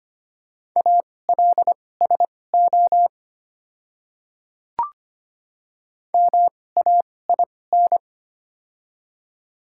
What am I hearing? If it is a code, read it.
Morse code, heard as a keyed tone, about 25 wpm, spaced as for 18 wpm, 700 Hz. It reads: ALSO MAIN
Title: 2x Character Spacing - Top 300 Words - 25wpm